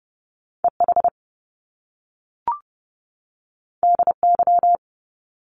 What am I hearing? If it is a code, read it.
Morse code, heard as a keyed tone, about 30 wpm, 700 Hz. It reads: EH DY
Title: Top Two Letter Combinations in Words 30wpm